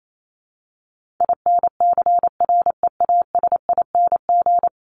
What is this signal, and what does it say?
Morse code, heard as a keyed tone, about 28 wpm, 700 Hz. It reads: INCREASING
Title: ING Suffix - Top 100 Words with Base Words 28wpm